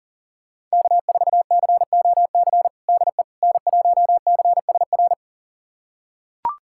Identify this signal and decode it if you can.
Morse code, heard as a keyed tone, about 37 wpm, 700 Hz. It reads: K4COC DE N1CSR
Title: USA Callsign DE Callsign 40wpm